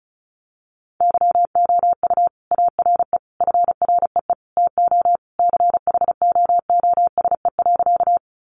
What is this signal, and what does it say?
Morse code, heard as a keyed tone, about 34 wpm, 700 Hz. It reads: YOU ARE FREE TO CHOOSE.
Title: Sentences from Top 700 Words 35wpm